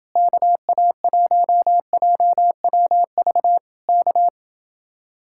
Morse code, heard as a keyed tone, about 29 wpm, 700 Hz. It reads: KA1JWV K